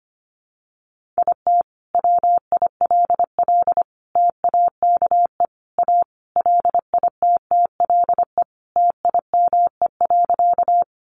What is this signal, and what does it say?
Morse code, heard as a keyed tone, about 25 wpm, 700 Hz. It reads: IT WILL TAKE A LITTLE TIME.
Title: Sentences from Top 200 Words 25wpm